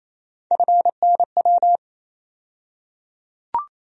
Morse code, heard as a keyed tone, about 28 wpm, 700 Hz. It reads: FNW
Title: USA Call Sign Postfixes - 3 Letters 28wpm